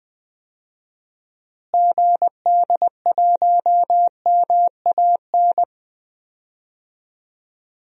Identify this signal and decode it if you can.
Morse code, heard as a keyed tone, about 21 wpm, 700 Hz. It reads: GD1MAN
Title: DXCC-100 Call Signs - 6 Characters - Repeated 3x - 20wpm